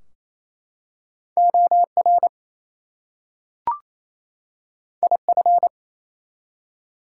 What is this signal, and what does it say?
Morse code, heard as a keyed tone, about 29 wpm, 700 Hz. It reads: OR IF
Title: Two Letter Words - 28wpm